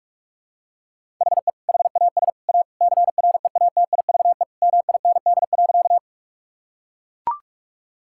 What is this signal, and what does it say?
Morse code, heard as a keyed tone, about 42 wpm, 700 Hz. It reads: HE HAS A CREATIVE MIND.
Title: Sentences from Top 1000 Words - Review Part 5 - 45wpm